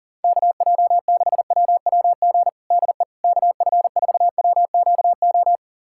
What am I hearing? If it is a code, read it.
Morse code, heard as a keyed tone, about 40 wpm, 700 Hz. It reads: KJ6WWG DE KF4WQO
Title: USA Callsign DE Callsign - 6 Characters - 40wpm